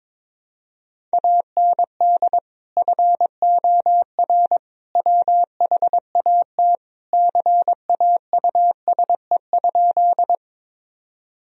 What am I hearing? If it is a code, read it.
Morse code, heard as a keyed tone, about 22 wpm, 700 Hz. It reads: AND FOR WHAT CAUSE?
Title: Sentences from Top 1000 Words - Review Part 1 - 22wpm